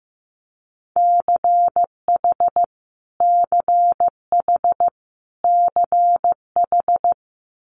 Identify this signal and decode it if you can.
Morse code, heard as a keyed tone, about 15 wpm, 700 Hz. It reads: CH CH CH
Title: Multi-letter Phonemes Repeated 3x 15wpm